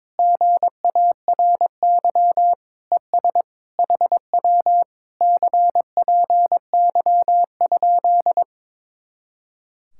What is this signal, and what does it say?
Morse code, heard as a keyed tone, about 22 wpm, 700 Hz. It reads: GARY ES HW CPY?
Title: Speed-Racing Common QSO Phrases - 22wpm